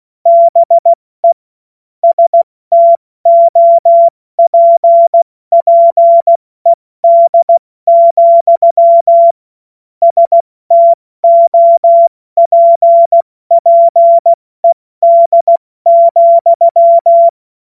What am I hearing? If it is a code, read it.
Morse code, heard as a keyed tone, about 16 wpm, 680 Hz. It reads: BE STOPPED, STOPPED,
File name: Morsekurs / Witze / Witz 27. September 2022 70 F80